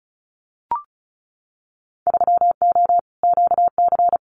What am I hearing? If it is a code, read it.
Morse code, heard as a keyed tone, about 36 wpm, 700 Hz. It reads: 3O QC
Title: Winter Field Day Exchanges Encore 35wpm